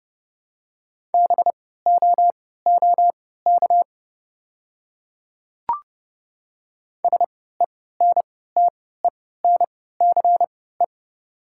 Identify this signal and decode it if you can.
Morse code, heard as a keyed tone, about 30 wpm, spaced as for 17 wpm, 700 Hz. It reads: BOOK SENTENCE
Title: 3x Character Spacing - Top 300 Words - 30wpm